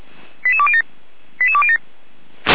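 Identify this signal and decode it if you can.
Five-tone selective call (ZVEI), heard as quick runs of five tones, tones of about 70 ms. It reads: 88207 88207
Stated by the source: ZVEI five-tone call 88169 received by SDR, from audio server of Feuerwehr Orlamünde (fire brigade).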